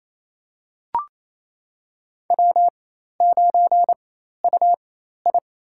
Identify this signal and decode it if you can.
Morse code, heard as a keyed tone, about 28 wpm, spaced as for 13 wpm, 700 Hz. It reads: W9UI
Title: USA Call Signs - 4x Character Spacing - 28wpm